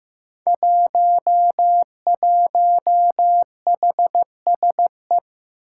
Morse code, heard as a keyed tone, about 15 wpm, 700 Hz. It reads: 11HSE
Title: Five Challenging Letter-Number Combinations 15wpm